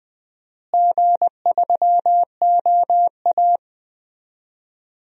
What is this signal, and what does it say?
Morse code, heard as a keyed tone, about 21 wpm, 700 Hz. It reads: G3OA
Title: DXCC-100 Call Signs - 4 Characters - Repeated 3x - 20wpm